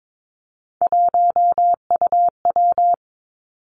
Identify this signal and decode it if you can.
Morse code, heard as a keyed tone, about 22 wpm, 700 Hz. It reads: 1UW
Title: Three Challenging Letter-Number Combinations 22wpm